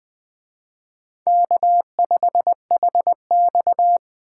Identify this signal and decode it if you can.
Morse code, heard as a keyed tone, about 20 wpm, 700 Hz. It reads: K5HX
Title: Speed Racing - USA 1x2 Call Signs 17wpm